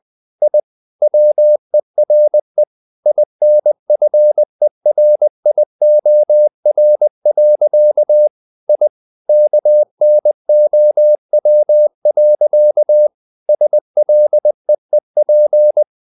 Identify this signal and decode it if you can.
Morse code, heard as a keyed tone, about 20 wpm, 600 Hz. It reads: I WERE INFERIOR. I KNOW. SLEEP
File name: the old man and the sea / the old man and the sea part 1 20wpm